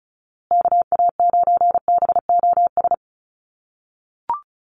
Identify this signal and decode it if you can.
Morse code, heard as a keyed tone, about 34 wpm, 700 Hz. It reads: KA9BOS
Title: USA Call Signs Repeated 3x 35wpm